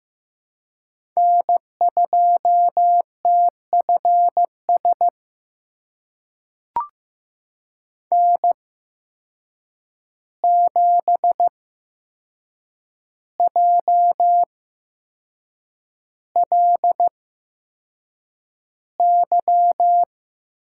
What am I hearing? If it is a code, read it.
Morse code, heard as a keyed tone, about 15 wpm, spaced as for 6 wpm, 700 Hz. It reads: N2TFS N7JLY
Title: USA Call Signs - 5 Characters - 8x Character Spacing - 15wpm